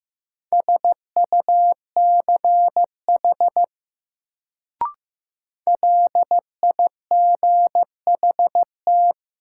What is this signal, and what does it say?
Morse code, heard as a keyed tone, about 15 wpm, 700 Hz. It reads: SUCH LIGHT
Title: Sentences from Top 200 Words - New Word Drill - 15wpm